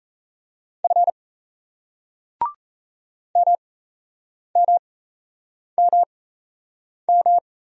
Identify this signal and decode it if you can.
Morse code, heard as a keyed tone, about 38 wpm, spaced as for 8 wpm, 700 Hz. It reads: F MMMM